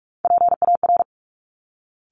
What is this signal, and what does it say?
Morse code, heard as a keyed tone, about 46 wpm, 690 Hz. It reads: PAR
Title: Top Three Letter Combinations in Words 45wpm